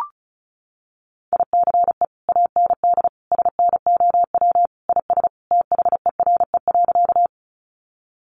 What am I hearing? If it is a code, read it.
Morse code, heard as a keyed tone, about 34 wpm, 700 Hz. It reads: ICE AND SNOW IS THERE.
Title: Sentences from Top 1000 Words 35wpm